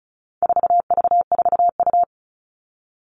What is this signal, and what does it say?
Morse code, heard as a keyed tone, about 34 wpm, 700 Hz. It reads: 4V4U
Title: All Cut Numbers - 4-Digit 35wpm